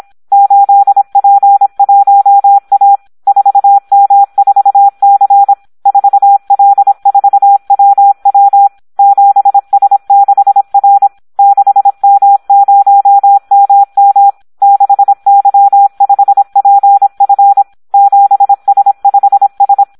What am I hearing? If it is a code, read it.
Morse code, heard as a keyed tone, about 26 wpm, 800 Hz. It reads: 8P1A 4M4C 4L4WW 7S6R 6M0MM 6Y5PF 7S5S
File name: Audio practica CW / SD 1 26